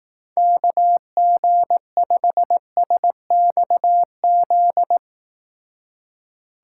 Morse code, heard as a keyed tone, about 18 wpm, 700 Hz. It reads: KG5SXZ
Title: Speed Racing - USA 2x3 Call Signs 15wpm